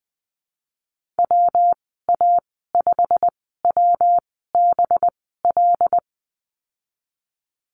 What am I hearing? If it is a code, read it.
Morse code, heard as a keyed tone, about 20 wpm, spaced as for 15 wpm, 700 Hz. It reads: WA5WBL
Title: 238 Lesson 38g - Letter q - Call Signs 20wpm